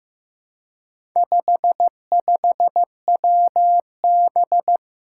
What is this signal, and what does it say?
Morse code, heard as a keyed tone, about 15 wpm, 700 Hz. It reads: 55WB
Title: Four Challenging Letter-Number Combinations 15wpm